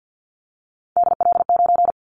Morse code, heard as a keyed tone, about 50 wpm, 700 Hz. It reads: DL9
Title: Callsign Prefixes for Top 50 DXCC - 50wpm